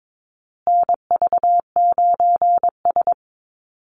Morse code, heard as a keyed tone, about 22 wpm, 700 Hz. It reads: NV9S
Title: USA Call Signs Repeated 3x 22wpm